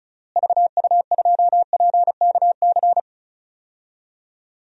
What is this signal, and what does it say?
Morse code, heard as a keyed tone, about 34 wpm, 700 Hz. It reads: VU2PKC
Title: DXCC-100 Call Signs - 6 Characters - Repeated 3x - 35wpm